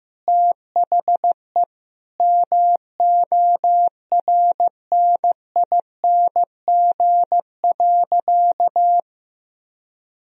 Morse code, heard as a keyed tone, about 15 wpm, 700 Hz. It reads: THE MORNING.
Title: Sentences from Top 1000 Words - Review Part 1 - 15wpm